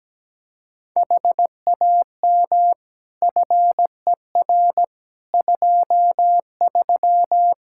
Morse code, heard as a keyed tone, about 17 wpm, 700 Hz. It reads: HAM FER 23
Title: Practice QSO - Exchange 3 - More Info - 17wpm